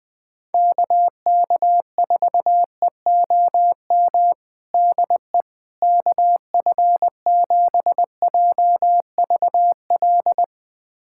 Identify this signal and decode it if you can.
Morse code, heard as a keyed tone, about 20 wpm, 700 Hz. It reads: KK4EOM DE KF7JVL
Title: USA Callsign DE Callsign - 6 Characters - 20wpm